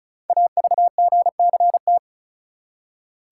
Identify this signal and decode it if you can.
Morse code, heard as a keyed tone, about 36 wpm, 700 Hz. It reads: AVGCT
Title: Code Groups - 5 Letter:Numbers - 35wpm